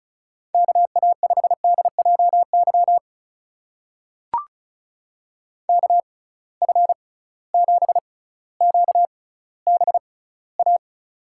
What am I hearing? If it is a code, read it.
Morse code, heard as a keyed tone, about 36 wpm, spaced as for 18 wpm, 700 Hz. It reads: KA5DJY KF7QBA
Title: USA Call Signs - 6x Character Spacing - 35wpm